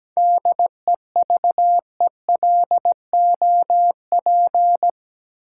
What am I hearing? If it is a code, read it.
Morse code, heard as a keyed tone, about 17 wpm, 700 Hz. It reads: DEVELOP
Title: Sentences from Top 1000 Words - Review Part 2 - New Word Drill - 17wpm